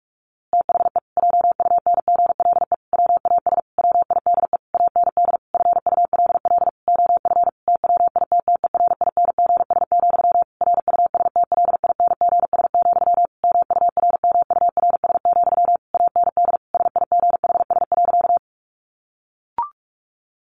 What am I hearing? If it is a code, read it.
Morse code, heard as a keyed tone, about 44 wpm, 700 Hz. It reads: THE JUNGLE WAS WIDE AND FULL OF TWITTERINGS, RUSTLINGS, MURMURS, AND SIGHS.
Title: A Sound Of Thunder 45wpm